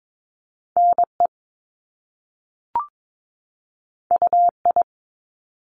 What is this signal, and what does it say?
Morse code, heard as a keyed tone, about 22 wpm, 700 Hz. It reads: N E U I
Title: Top Two Letter Combinations in Words 22wpm